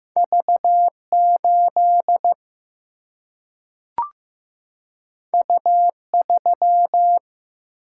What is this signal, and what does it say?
Morse code, heard as a keyed tone, about 15 wpm, 700 Hz. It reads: V8 U3
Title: Single-Letter Single-Number 15wpm